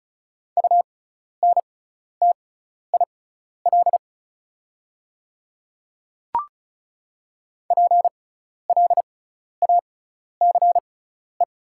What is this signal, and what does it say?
Morse code, heard as a keyed tone, about 36 wpm, spaced as for 12 wpm, 700 Hz. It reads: UNTIL PLACE
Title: 6x Character Spacing - Top 500 Words - 35wpm